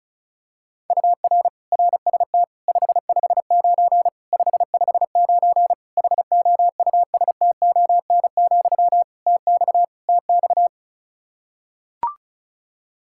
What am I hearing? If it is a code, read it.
Morse code, heard as a keyed tone, about 34 wpm, 700 Hz. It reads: UR RST 559 559 HOUSTON, TX TX
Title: K1USN SST QSO Practice 35wpm